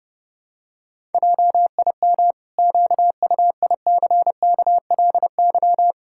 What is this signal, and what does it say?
Morse code, heard as a keyed tone, about 32 wpm, 700 Hz. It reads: JIM QUICKLY
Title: Pangram Challenge v2 - 30wpm